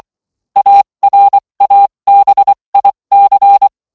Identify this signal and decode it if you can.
Morse code, heard as a keyed tone, about 24 wpm, spaced as for 20 wpm, 800 Hz. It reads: ARABIC